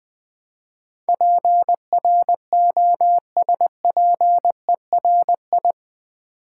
Morse code, heard as a keyed tone, about 20 wpm, 700 Hz. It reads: PROSPERI
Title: Build A Word - 4 Letters At A Time - 10-Letter Words - 20wpm